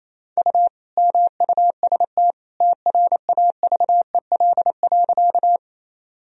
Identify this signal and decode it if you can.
Morse code, heard as a keyed tone, about 28 wpm, 700 Hz. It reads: U MUST TRAVEL.
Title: Sentences from Top 400 Words and Abbreviations 28wpm